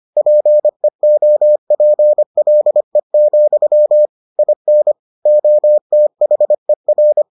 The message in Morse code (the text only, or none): PEOPLE, IN OTHER